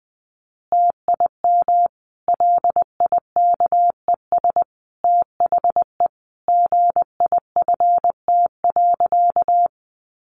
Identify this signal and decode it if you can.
Morse code, heard as a keyed tone, about 20 wpm, 700 Hz. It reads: TIM LIKES THE GIFT.
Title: Easy Phrases 20wpm